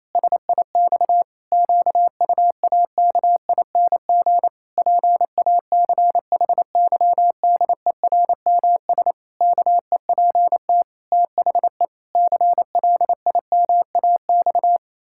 Morse code, heard as a keyed tone, about 28 wpm, 700 Hz. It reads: SIX QUAKING PACHYDERMS KEPT THE CLIMAX
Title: Pangram Challenge v2 - 28wpm